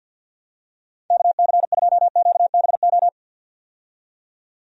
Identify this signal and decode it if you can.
Morse code, heard as a keyed tone, about 52 wpm, 700 Hz. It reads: KC2QBO
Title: USA Call Signs - 6 Characters - 50wpm